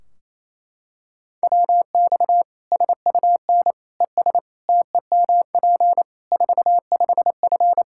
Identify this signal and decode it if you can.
Morse code, heard as a keyed tone, about 28 wpm, 700 Hz. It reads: WX SUN ES TEMP 45F
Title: Practice QSO - Exchange 2 - Info - 28wpm